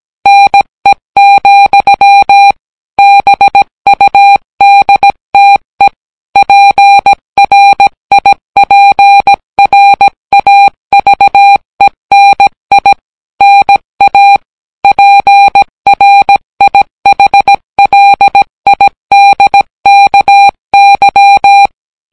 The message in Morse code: NE, BUDTE PRIPRAVENI NA PRIHLIDKY